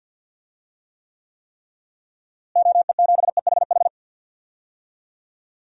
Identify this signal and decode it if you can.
Morse code, heard as a keyed tone, about 47 wpm, 700 Hz. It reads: OE7EHH
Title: DXCC-100 Call Signs - 6 Characters - Repeated 3x - 50wpm